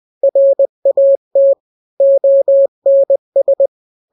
Morse code, heard as a keyed tone, about 20 wpm, 550 Hz. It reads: RAT ONS
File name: n-grams / 3-grams 20 18 WPM